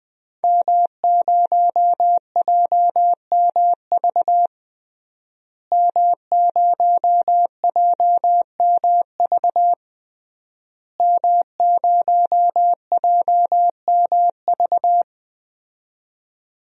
Morse code, heard as a keyed tone, about 21 wpm, 700 Hz. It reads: M0JMV M0JMV M0JMV
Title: DXCC-100 Call Signs - 5 Characters - Repeated 3x - 20wpm